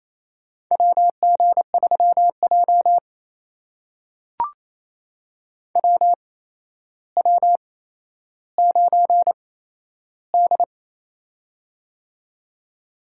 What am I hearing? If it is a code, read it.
Morse code, heard as a keyed tone, about 27 wpm, spaced as for 12 wpm, 700 Hz. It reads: WG3J WW9D
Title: USA Call Signs - 4 Characters - 8x Character Spacing - 28wpm